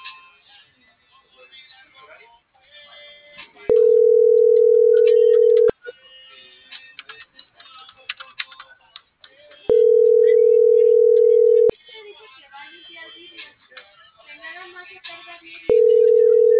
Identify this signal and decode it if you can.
Ringback tone (North America), 440 + 480 Hz.